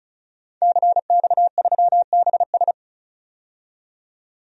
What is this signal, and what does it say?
Morse code, heard as a keyed tone, about 36 wpm, 700 Hz. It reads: CX3BS
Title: DXCC-100 Call Signs - 5 Characters - 35wpm